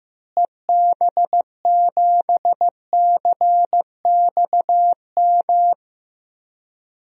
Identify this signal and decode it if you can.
Morse code, heard as a keyed tone, about 15 wpm, 700 Hz. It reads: EB7CXM
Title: DXCC-100 Call Signs - 6 Characters - Repeated 2x - 15wpm